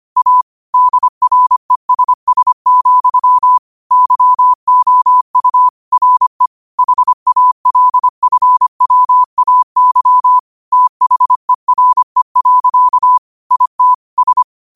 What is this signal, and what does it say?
Morse code, heard as a keyed tone, about 25 wpm, 1000 Hz. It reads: A DRESS, YOU RE HALFWAY THERE. IT S
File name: qotdfiles / 25wpm